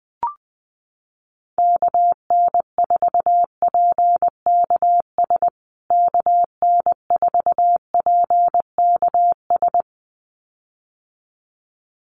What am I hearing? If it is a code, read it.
Morse code, heard as a keyed tone, about 20 wpm, 700 Hz. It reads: KN4PKS KN4PKS
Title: USA Call Signs Repeated 2x 20wpm